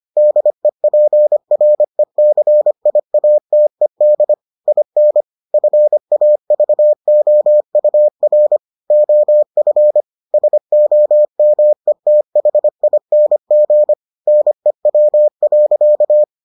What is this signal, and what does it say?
Morse code, heard as a keyed tone, about 25 wpm, 600 Hz. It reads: DEPRECIATED IN FAVOUR OF SOMETHING NEW.